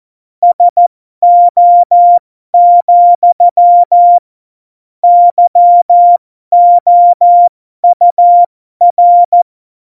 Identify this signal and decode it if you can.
Morse code, heard as a keyed tone, about 14 wpm, spaced as for 12 wpm, 700 Hz. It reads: SO, YOUR
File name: Morsekurs / Witze